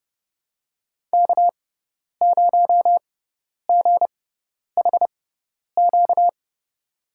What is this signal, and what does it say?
Morse code, heard as a keyed tone, about 31 wpm, spaced as for 10 wpm, 700 Hz. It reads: K0GHQ
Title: USA Call Signs - 5 Characters - 6x Character Spacing - 30wpm